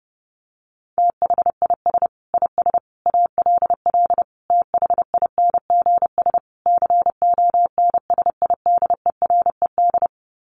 Morse code, heard as a keyed tone, about 30 wpm, 700 Hz. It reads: THIS IS ALL THINGS CONSIDERED
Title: Sets of 5 Words 30wpm